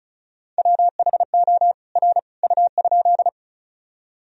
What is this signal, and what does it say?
Morse code, heard as a keyed tone, about 36 wpm, 700 Hz. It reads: WHO R U?
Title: Sentences from Top 100 Words and Abbreviations 35wpm